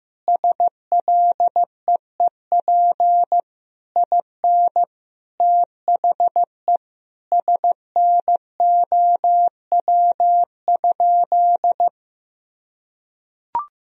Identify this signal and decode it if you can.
Morse code, heard as a keyed tone, about 15 wpm, 700 Hz. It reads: SLEEP IN THE SNOW?
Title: Sentences from Top 1000 Words - Review Part 3 - 15wpm